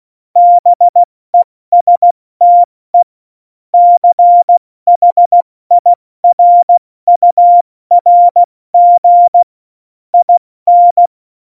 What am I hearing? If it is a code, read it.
Morse code, heard as a keyed tone, about 16 wpm, spaced as for 14 wpm, 700 Hz. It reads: BESTE CHIRURG IN
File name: Morsekurs / Witze / Witz 01. Dezember 2020, 70 F80